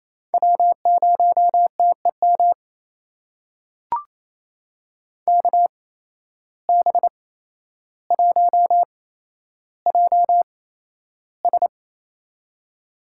Morse code, heard as a keyed tone, about 28 wpm, spaced as for 12 wpm, 700 Hz. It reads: W0TEM KB1JS
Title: USA Call Signs - 5 Characters - 8x Character Spacing - 28wpm